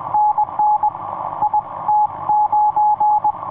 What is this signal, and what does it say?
Morse code, heard as a keyed tone, about 20 wpm, 840 Hz, with radio noise. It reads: NN IT9
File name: kp4ms it9esw etc 30m cw 27jan2010 2226